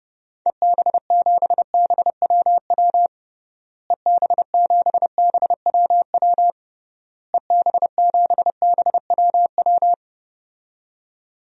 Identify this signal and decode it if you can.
Morse code, heard as a keyed tone, about 32 wpm, 700 Hz. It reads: EB7BWW EB7BWW EB7BWW
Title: DXCC-100 Call Signs - 6 Characters - Repeated 3x - 30wpm